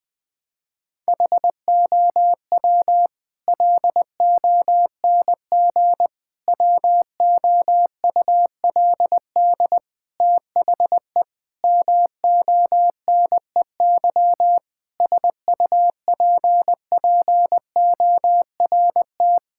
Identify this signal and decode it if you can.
Morse code, heard as a keyed tone, about 20 wpm, 700 Hz. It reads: HOW LONG WOULD THE MONEY SUPPORT
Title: Sentences from Top 1000 Words - Review Part 5 - 20wpm